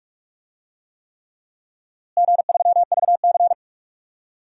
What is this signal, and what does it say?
Morse code, heard as a keyed tone, about 47 wpm, 700 Hz. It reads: G3VC
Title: DXCC-100 Call Signs - 4 Characters - Repeated 3x - 45wpm